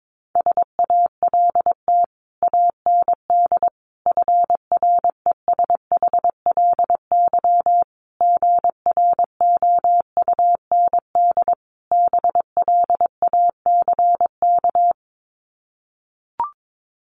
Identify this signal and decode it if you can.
Morse code, heard as a keyed tone, about 22 wpm, 700 Hz. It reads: SALT AND FRESHLY GROUND BLACK
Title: Sets of 5 Words Encore 22wpm